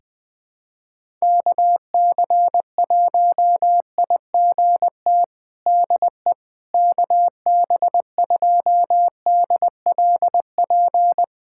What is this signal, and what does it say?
Morse code, heard as a keyed tone, about 21 wpm, 700 Hz. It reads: KC1IGT DE KB2DLP
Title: USA Callsign DE Callsign - 6 Characters - 20wpm